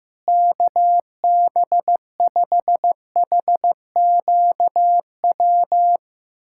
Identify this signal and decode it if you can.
Morse code, heard as a keyed tone, about 15 wpm, 700 Hz. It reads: KB5HQW
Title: USA Callsign DE Callsign 15wpm